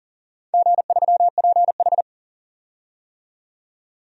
Morse code, heard as a keyed tone, about 41 wpm, 700 Hz. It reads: G3PH
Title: DXCC-100 Call Signs - 4 Characters - 40wpm